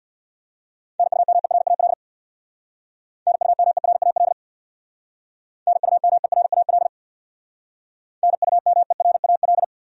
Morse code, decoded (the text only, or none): NUMERAL NUMERAL NUMERAL NUMERAL